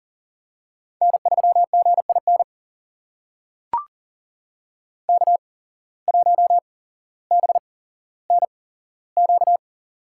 Morse code, heard as a keyed tone, about 40 wpm, spaced as for 17 wpm, 700 Hz. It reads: N3GIN K1BNQ
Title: USA Call Signs - 5 Characters - 8x Character Spacing - 40wpm